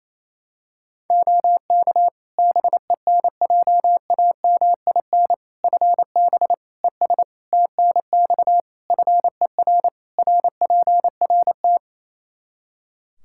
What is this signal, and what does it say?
Morse code, heard as a keyed tone, about 28 wpm, 700 Hz. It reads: OK BENJAMIN FB ES TNX FER RPRT
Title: K1USN SST QSO Practice 28wpm